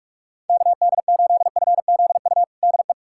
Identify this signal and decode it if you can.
Morse code, heard as a keyed tone, about 42 wpm, 700 Hz. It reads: KD8FZU DE
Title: USA Callsign DE Callsign 45wpm